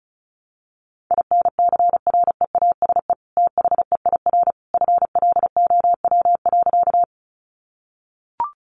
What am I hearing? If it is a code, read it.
Morse code, heard as a keyed tone, about 34 wpm, 700 Hz. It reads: INCREASE THEIR FLOW.
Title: Sentences from Top 1200 Words 35wpm